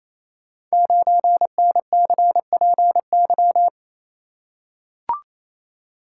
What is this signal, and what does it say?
Morse code, heard as a keyed tone, about 29 wpm, 700 Hz. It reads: 9NCPY
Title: Code Groups - 5 Letter:Numbers - 28wpm